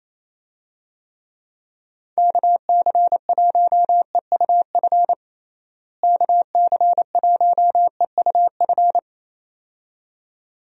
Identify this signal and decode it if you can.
Morse code, heard as a keyed tone, about 28 wpm, 700 Hz. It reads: KC1EUF KC1EUF
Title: USA Call Signs - 6 Characters - Repeated 2x - 28wpm